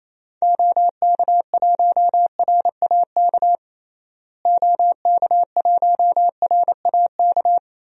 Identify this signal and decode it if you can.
Morse code, heard as a keyed tone, about 27 wpm, 700 Hz. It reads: OK1RAK OK1RAK
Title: DXCC-100 Call Signs - 6 Characters - Repeated 3x - 28wpm